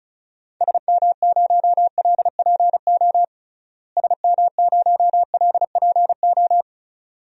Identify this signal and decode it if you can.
Morse code, heard as a keyed tone, about 36 wpm, 700 Hz. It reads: SM0LPO SM0LPO
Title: DXCC-100 Call Signs - 6 Characters - Repeated 2x - 35wpm